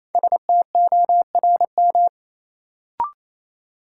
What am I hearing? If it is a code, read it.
Morse code, heard as a keyed tone, about 28 wpm, 700 Hz. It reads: STORM
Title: Sentences from Top 1500 Words - New Word Drill Alt - 28wpm